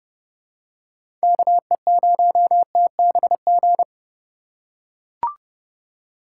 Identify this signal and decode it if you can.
Morse code, heard as a keyed tone, about 31 wpm, 700 Hz. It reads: KE0TBG
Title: USA Call Signs - 6 Characters - 4x Character Spacing - 30wpm